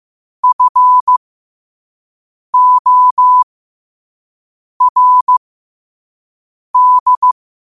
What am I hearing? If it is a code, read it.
Morse code, heard as a keyed tone, about 15 wpm, spaced as for 5 wpm, 1000 Hz. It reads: FORD
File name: qotdfiles / 5wpm